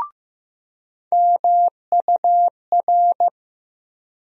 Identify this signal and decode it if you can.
Morse code, heard as a keyed tone, about 15 wpm, 700 Hz. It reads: MUR
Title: USA Call Sign Postfixes - 3 Letters 15wpm